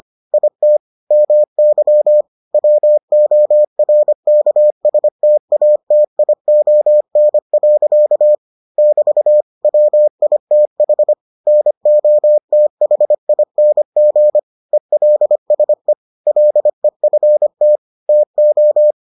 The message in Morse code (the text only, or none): IT MY WORKSTATION. <BT> WITH NOTHING ELSE LEFT TO